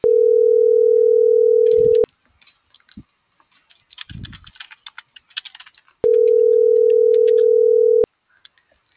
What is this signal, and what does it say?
Ringback tone (North America), 440 + 480 Hz.